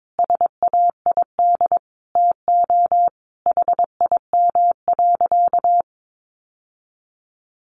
Morse code, heard as a keyed tone, about 22 wpm, 700 Hz. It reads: SAID TO HIM.
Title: Sentences from Top 800 Words 22wpm